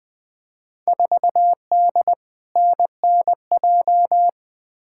Morse code, heard as a keyed tone, about 20 wpm, 700 Hz. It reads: 4D NNJ
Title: Field Day Exchanges Encore - 20wpm